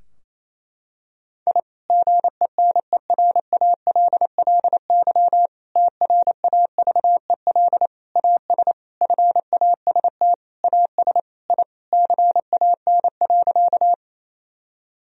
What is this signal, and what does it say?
Morse code, heard as a keyed tone, about 27 wpm, 700 Hz. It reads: I GENERALLY TRAVEL AS FAST AS I CAN.
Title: Sentences from Top 1000 Words - Review Part 4 - 28wpm